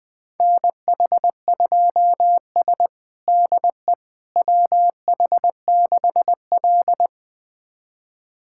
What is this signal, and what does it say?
Morse code, heard as a keyed tone, about 20 wpm, 700 Hz. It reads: NH2S DE WH6L
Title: USA Callsign DE Callsign - 4 Characters - 20wpm